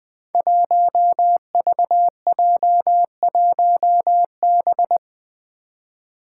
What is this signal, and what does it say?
Morse code, heard as a keyed tone, about 21 wpm, 700 Hz. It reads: 1VJ1B
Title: Five Challenging Letter-Number Combinations 20wpm